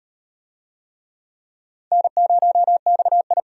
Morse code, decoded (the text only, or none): N0XI